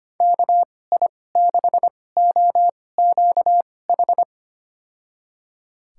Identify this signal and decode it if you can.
Morse code, heard as a keyed tone, about 25 wpm, spaced as for 18 wpm, 700 Hz. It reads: KI6OQH
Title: USA Call Signs - 6 Characters - 2x Character Spacing - 25wpm